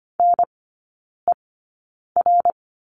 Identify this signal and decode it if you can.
Morse code, heard as a keyed tone, about 25 wpm, spaced as for 9 wpm, 700 Hz. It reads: NER
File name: Lesson 14e - Number 2 - Three Characters 25wpm